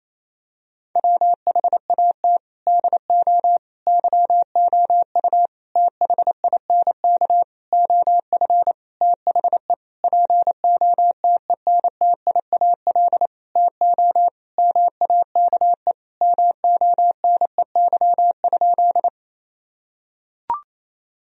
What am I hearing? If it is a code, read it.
Morse code, heard as a keyed tone, about 27 wpm, 700 Hz. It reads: WHAT DO YOU THINK OF THE POTENTIAL TO MAKE MONEY?
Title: Sentences from Top 1000 Words - Review Part 4 - 28wpm